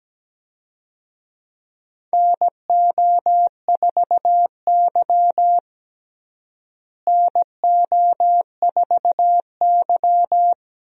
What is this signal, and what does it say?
Morse code, heard as a keyed tone, about 17 wpm, 700 Hz. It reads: NO4Y NO4Y
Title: USA Call Signs - 4 Characters - Repeated 2x - 17wpm